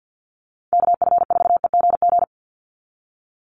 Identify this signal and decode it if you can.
Morse code, heard as a keyed tone, about 50 wpm, 700 Hz. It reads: KF4EGG
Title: USA Call Signs 50wpm